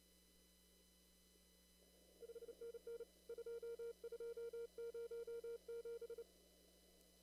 Morse code, heard as a keyed tone, about 27 wpm, 470 Hz. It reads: 5NN 2207
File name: CW / kt5j wpx cw-op-k5pi